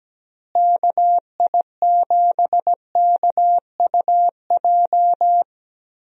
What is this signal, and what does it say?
Morse code, heard as a keyed tone, about 17 wpm, 700 Hz. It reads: KI7KUJ
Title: USA Call Signs - 6 Characters - Repeated 3x - 17wpm